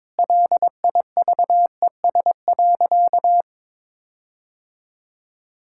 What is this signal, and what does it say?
Morse code, heard as a keyed tone, about 22 wpm, 700 Hz. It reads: LIVES.
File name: Difficult Sentences with Hints - 22wpm